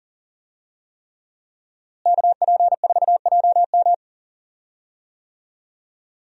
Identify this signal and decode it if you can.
Morse code, heard as a keyed tone, about 40 wpm, 700 Hz. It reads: KP4JM
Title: DXCC-100 Call Signs - 5 Characters - Repeated 3x - 40wpm